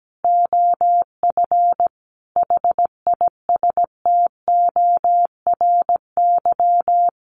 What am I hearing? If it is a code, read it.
Morse code, heard as a keyed tone, about 17 wpm, 700 Hz. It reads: OF HISTORY
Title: Sentences from Top 900 Words 17wpm